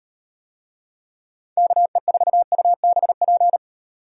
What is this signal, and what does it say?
Morse code, heard as a keyed tone, about 38 wpm, 700 Hz. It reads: KE4UBP